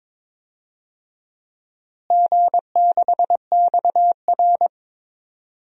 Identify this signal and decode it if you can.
Morse code, heard as a keyed tone, about 22 wpm, 700 Hz. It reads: G6XR
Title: DXCC-100 Call Signs - 4 Characters - Repeated 2x - 22wpm